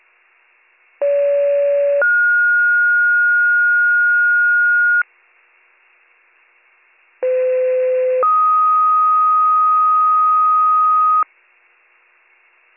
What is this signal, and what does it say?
Two-tone page, heard as a one-second tone, then a three-second tone, about 570 then 1430 Hz, about 510 then 1250 Hz. It is heard from 1.0 to 11.2 s.